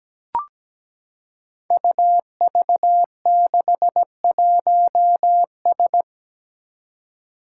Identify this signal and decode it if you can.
Morse code, heard as a keyed tone, about 17 wpm, 700 Hz. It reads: UV61S
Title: Five Challenging Letter-Number Combinations 17wpm